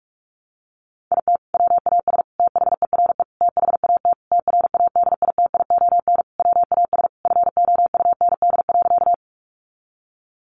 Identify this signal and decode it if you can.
Morse code, heard as a keyed tone, about 44 wpm, 700 Hz. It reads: IT WAS THERE THAT TRADITION WAS FOUND.
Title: Sentences from Top 1500 Words 45wpm